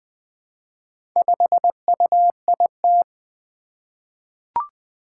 Morse code, heard as a keyed tone, about 20 wpm, 700 Hz. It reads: 5UIT
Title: Four Challenging Letter-Number Combinations 20wpm